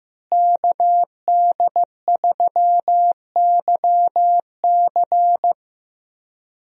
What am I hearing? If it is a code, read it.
Morse code, heard as a keyed tone, about 15 wpm, 700 Hz. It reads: KD3YC
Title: USA Call Signs - 5 Characters - Repeated 3x - 15wpm